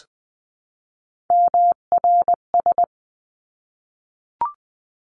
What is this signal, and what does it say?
Morse code, heard as a keyed tone, about 20 wpm, 700 Hz. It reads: MRS
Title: 259 Lesson 42d - Words 18-20wpm